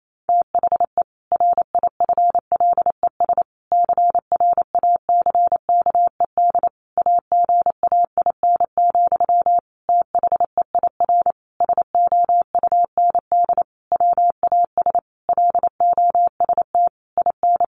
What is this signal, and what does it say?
Morse code, heard as a keyed tone, about 28 wpm, 700 Hz. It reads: THE RIFLES CRACKED AGAIN, THEIR SOUND WAS LOST IN
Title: A Sound Of Thunder 28wpm